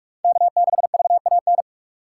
Morse code, heard as a keyed tone, about 47 wpm, 700 Hz. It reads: K6VAN
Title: USA Call Signs - 5 Characters - 2x Character Spacing - 45wpm